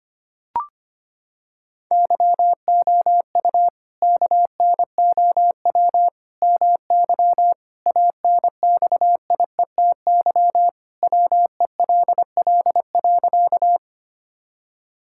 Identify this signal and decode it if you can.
Morse code, heard as a keyed tone, about 25 wpm, 700 Hz. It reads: YOU KNOW MY ANXIETY WELL.
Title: Sentences from Top 1600 Words 25wpm